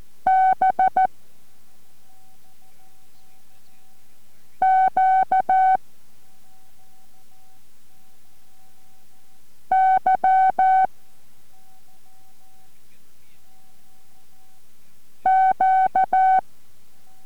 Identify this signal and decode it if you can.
Morse code, heard as a keyed tone, about 14 wpm, 760 Hz. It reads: B Q Y Q